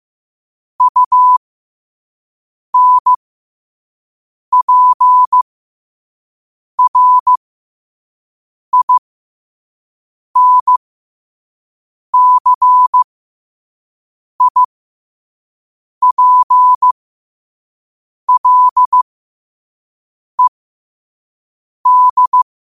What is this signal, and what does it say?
Morse code, heard as a keyed tone, about 15 wpm, spaced as for 5 wpm, 1000 Hz. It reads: UNPRINCIPLED
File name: qotdfiles / 5wpm